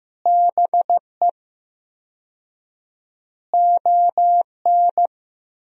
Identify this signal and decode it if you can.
Morse code, heard as a keyed tone, about 15 wpm, 700 Hz. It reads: BE ON
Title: 4x Word-Spacing - Sentences from Top 300 Words - 15wpm